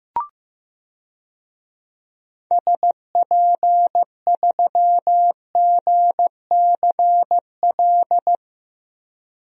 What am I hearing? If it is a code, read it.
Morse code, heard as a keyed tone, about 15 wpm, 700 Hz. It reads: SP3GCL
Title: DXCC-100 Call Signs - 6 Characters - Repeated 2x - 15wpm